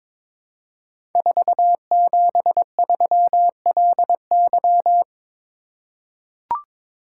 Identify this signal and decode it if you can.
Morse code, heard as a keyed tone, about 22 wpm, 700 Hz. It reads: 473LY
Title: Code Groups - 5 Letter:Numbers - 22wpm